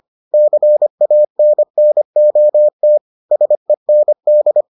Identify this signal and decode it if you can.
Morse code, heard as a keyed tone, about 25 wpm, 600 Hz. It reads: CANNOT SEND